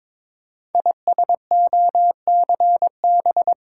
Morse code, heard as a keyed tone, about 22 wpm, 700 Hz. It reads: ISOCB
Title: Code Groups - 5 Letters - 22wpm